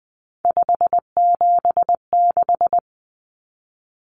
Morse code, heard as a keyed tone, about 20 wpm, 700 Hz. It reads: 576
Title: Numbers - 3-Digit 20wpm